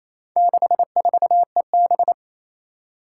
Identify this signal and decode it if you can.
Morse code, heard as a keyed tone, about 28 wpm, 700 Hz. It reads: 64EB
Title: Four Challenging Letter-Number Combinations 28wpm